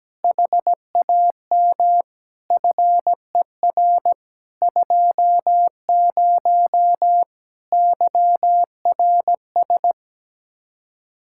Morse code, heard as a keyed tone, about 17 wpm, 700 Hz. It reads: HAM FER 20 YRS
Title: Practice QSO - Exchange 3 - More Info - 17wpm